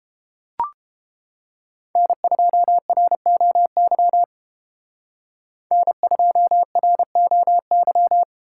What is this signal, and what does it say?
Morse code, heard as a keyed tone, about 31 wpm, 700 Hz. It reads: N2ROY N2ROY